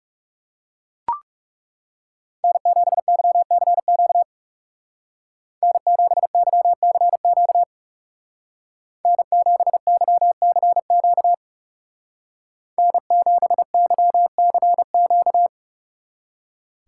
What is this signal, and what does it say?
Morse code, heard as a keyed tone, about 36 wpm, 700 Hz. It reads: N7YCQ N7YCQ N7YCQ N7YCQ
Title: Speed Racing - USA 1x3 Call Signs 30wpm